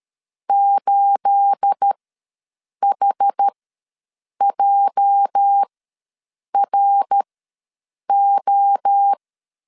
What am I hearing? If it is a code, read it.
Morse code, heard as a keyed tone, about 13 wpm, spaced as for 7 wpm, 800 Hz. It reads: 8HJRO
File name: LEZIONI / 10 SIGNS / 10A SLASH ws 12wpm 3443